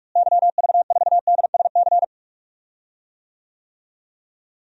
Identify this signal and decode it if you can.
Morse code, heard as a keyed tone, about 41 wpm, 700 Hz. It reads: YV4DSC
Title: DXCC-100 Call Signs - 6 Characters - Repeated 2x - 45wpm